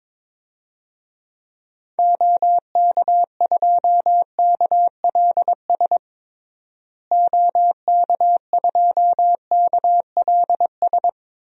Morse code, heard as a keyed tone, about 22 wpm, 700 Hz. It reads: OK2KLS OK2KLS
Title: DXCC-100 Call Signs - 6 Characters - Repeated 3x - 22wpm